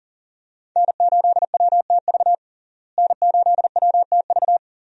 Morse code, heard as a keyed tone, about 40 wpm, 700 Hz. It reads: N8WTV N8WTV
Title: USA Call Signs - 5 Characters - Repeated 2x - 40wpm